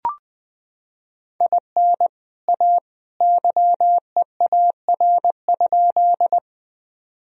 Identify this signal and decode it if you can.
Morse code, heard as a keyed tone, about 21 wpm, 700 Hz. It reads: IN A YEAR?
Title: Sentences from Top 1000 Words - Review Part 1 - 20wpm